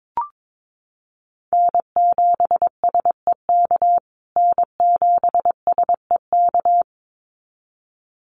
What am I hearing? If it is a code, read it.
Morse code, heard as a keyed tone, about 22 wpm, 700 Hz. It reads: N7SEK N7SEK
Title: USA Call Signs Repeated 2x 22wpm